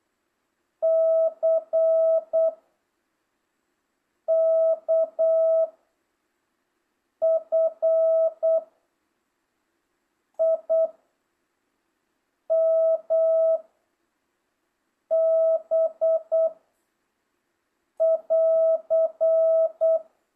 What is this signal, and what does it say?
Morse code, heard as a keyed tone, about 8 wpm, 660 Hz. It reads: C K F I M B <AR>